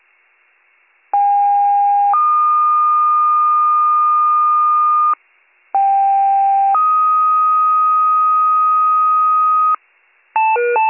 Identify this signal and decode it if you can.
Two-tone page, heard as a one-second tone, then a three-second tone, about 810 then 1220 Hz, about 790 then 1250 Hz. It is heard from 1.1 to 9.8 s.